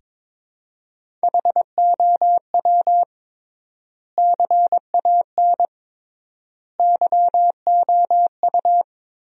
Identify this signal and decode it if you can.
Morse code, heard as a keyed tone, about 22 wpm, 700 Hz. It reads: HOW CAN YOU
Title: 3x Word-Spacing - Sentences from Top 100 Words - 22wpm